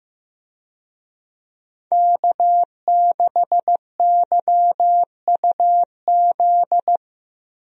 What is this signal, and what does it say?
Morse code, heard as a keyed tone, about 15 wpm, 700 Hz. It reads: K6YUZ
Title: USA Call Signs - 5 Characters - Repeated 2x - 15wpm